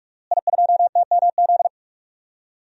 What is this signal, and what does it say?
Morse code, heard as a keyed tone, about 41 wpm, 700 Hz. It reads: I2TMZ
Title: DXCC-100 Call Signs - 5 Characters - Repeated 3x - 45wpm